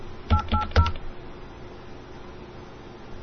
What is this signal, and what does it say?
DTMF, heard as telephone touch tones, tones of about 100 ms. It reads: 880